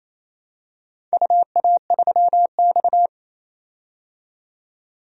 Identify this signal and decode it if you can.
Morse code, heard as a keyed tone, about 28 wpm, 700 Hz. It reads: UA3X
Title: DXCC-100 Call Signs - 4 Characters - 28wpm